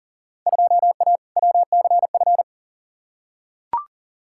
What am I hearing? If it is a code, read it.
Morse code, heard as a keyed tone, about 40 wpm, 700 Hz. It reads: 2A WCF
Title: Field Day Exchanges Encore - 40wpm